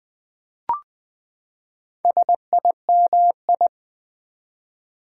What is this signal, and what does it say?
Morse code, heard as a keyed tone, about 20 wpm, 700 Hz. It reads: SIMI
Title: Build A Word - 4 Letters At A Time - 10-Letter Words - 20wpm